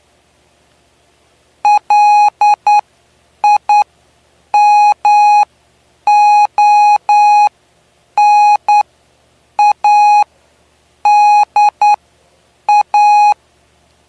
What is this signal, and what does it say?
Morse code, heard as a keyed tone, about 10 wpm, spaced as for 7 wpm, 830 Hz. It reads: LIMONADA
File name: cw / py2lcd / 09-PY2LCD-Li��o 9